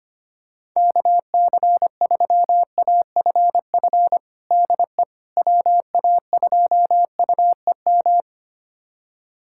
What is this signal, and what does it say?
Morse code, heard as a keyed tone, about 25 wpm, 700 Hz. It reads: KC3AFF DE WA2UEM
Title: USA Callsign DE Callsign - 6 Characters - 25wpm